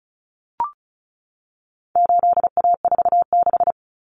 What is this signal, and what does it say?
Morse code, heard as a keyed tone, about 34 wpm, 700 Hz. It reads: 8A46
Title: Common Cut Numbers - 4-Digit 35wpm